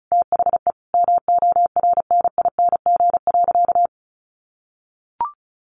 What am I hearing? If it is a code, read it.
Morse code, heard as a keyed tone, about 34 wpm, 700 Hz. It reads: THE MORNING.
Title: Sentences from Top 1500 Words 35wpm